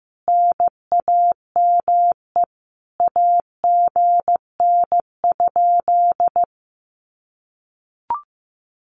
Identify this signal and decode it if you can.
Morse code, heard as a keyed tone, about 15 wpm, 700 Hz. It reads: NAME AGN?
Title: Common QSO Phrases Encore 15wpm